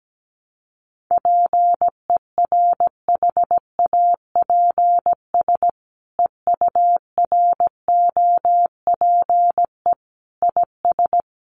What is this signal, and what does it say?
Morse code, heard as a keyed tone, about 17 wpm, 700 Hz. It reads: PERHAPS EUROPE IS
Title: A Sound Of Thunder 17wpm